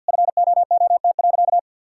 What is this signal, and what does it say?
Morse code, heard as a keyed tone, about 52 wpm, 700 Hz. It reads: FOOT.
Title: Sentences from Top 1000 Words - Review Part 6 - 50wpm